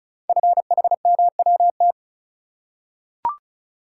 Morse code, heard as a keyed tone, about 36 wpm, 690 Hz. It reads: FHMWT
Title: Code Groups - 5 Letters - 35wpm